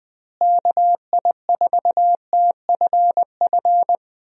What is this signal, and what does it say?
Morse code, heard as a keyed tone, about 20 wpm, 700 Hz. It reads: KI4TFF
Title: USA Call Signs - 6 Characters - Repeated 3x - 20wpm